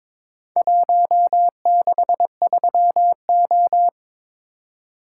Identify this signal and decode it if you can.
Morse code, heard as a keyed tone, about 22 wpm, 700 Hz. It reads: 163O
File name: practice / 22wpm / Four Letter-Number Combinations - 22wpm